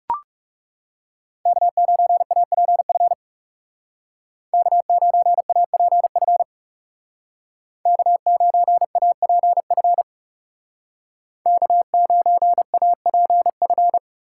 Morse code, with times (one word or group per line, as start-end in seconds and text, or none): 1.5-3.1 K9APF
4.5-6.4 K9APF
7.9-10.0 K9APF
11.5-14.0 K9APF